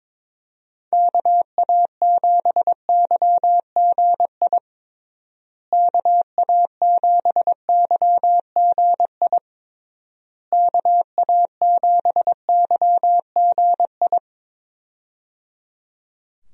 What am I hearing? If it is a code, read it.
Morse code, heard as a keyed tone, about 22 wpm, 700 Hz. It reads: KA7YGI KA7YGI KA7YGI
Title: USA Call Signs - 6 Characters - Repeated 3x - 22wpm